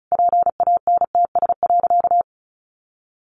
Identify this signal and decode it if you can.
Morse code, heard as a keyed tone, about 34 wpm, 700 Hz. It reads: PANTS.
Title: Additional Easy Phrases 35wpm